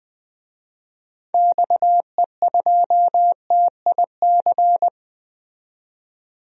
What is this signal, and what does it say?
Morse code, heard as a keyed tone, about 20 wpm, 700 Hz. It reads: XE2TIC
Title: DXCC-100 Call Signs - 6 Characters - 20wpm